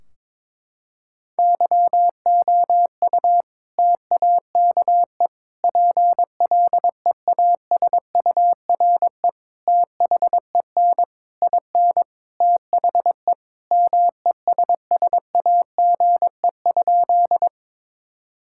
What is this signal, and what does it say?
Morse code, heard as a keyed tone, about 22 wpm, 700 Hz. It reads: YOU TAKE PLEASURE THEN IN THE MESSAGE?
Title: Sentences from Top 1600 Words 22wpm